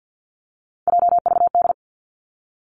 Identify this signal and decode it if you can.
Morse code, heard as a keyed tone, about 50 wpm, 700 Hz. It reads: JVD
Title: Three Challenging Letter-Number Combinations 50wpm